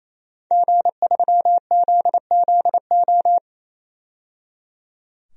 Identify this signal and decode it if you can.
Morse code, heard as a keyed tone, about 28 wpm, 700 Hz. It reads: G3ZZO
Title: DXCC-100 Call Signs - 5 Characters - 28wpm